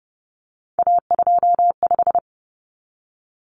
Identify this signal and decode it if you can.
Morse code, heard as a keyed tone, about 30 wpm, 700 Hz. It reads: A25
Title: Common Cut Numbers - 3-Digit 30wpm